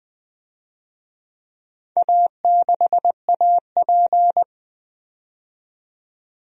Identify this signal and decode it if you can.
Morse code, heard as a keyed tone, about 20 wpm, 700 Hz. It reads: A6AP
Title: DXCC-100 Call Signs - 4 Characters - 20wpm